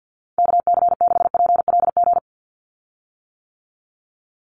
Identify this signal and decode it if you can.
Morse code, heard as a keyed tone, about 50 wpm, 700 Hz. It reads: KC6PLG
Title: USA Call Signs 50wpm